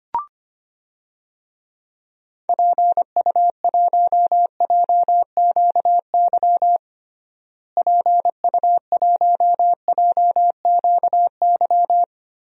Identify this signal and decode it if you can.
Morse code, heard as a keyed tone, about 25 wpm, 700 Hz. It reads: PU1JQY PU1JQY
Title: DXCC-100 Call Signs - 6 Characters - Repeated 3x - 25wpm